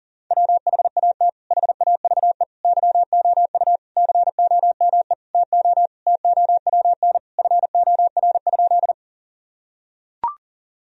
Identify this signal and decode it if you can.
Morse code, heard as a keyed tone, about 40 wpm, 700 Hz. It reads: WHAT HAVE YOU COME TO TOWN FOR?
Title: Sentences from Top 1000 Words - Review Part 2 - 40wpm